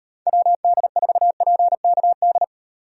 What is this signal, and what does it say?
Morse code, heard as a keyed tone, about 38 wpm, 700 Hz. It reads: WD4PKD